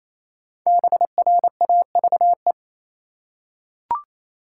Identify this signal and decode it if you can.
Morse code, heard as a keyed tone, about 28 wpm, 700 Hz. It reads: BRAVE
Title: Five Letter Words - 28wpm